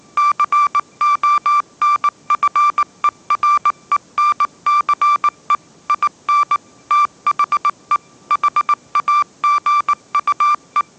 Morse code, heard as a keyed tone, about 19 wpm, 1180 Hz. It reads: CONFERENCE IN THE HAGUE
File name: mrx083 20wpm